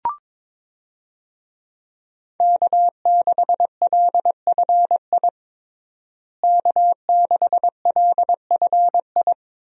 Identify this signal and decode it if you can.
Morse code, heard as a keyed tone, about 22 wpm, 700 Hz. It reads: K6LFI K6LFI
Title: USA Call Signs - 5 Characters - Repeated 3x - 22wpm